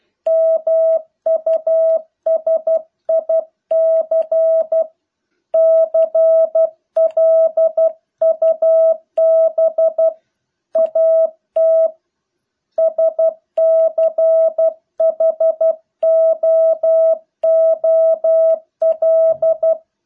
Morse code, heard as a keyed tone, about 12 wpm, 650 Hz. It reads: MUSIC CLUB AT SCHOOL